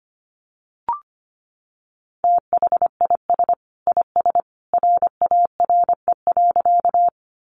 Morse code, heard as a keyed tone, about 25 wpm, 700 Hz. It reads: THIS IS RARE.
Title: Sentences from Top 1100 Words 25wpm